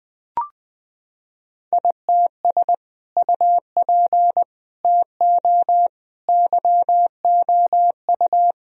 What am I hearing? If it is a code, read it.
Morse code, heard as a keyed tone, about 20 wpm, 700 Hz. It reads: ITS UP TO YOU
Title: Sentences from Top 1000 Words - Review Part 2 - 20wpm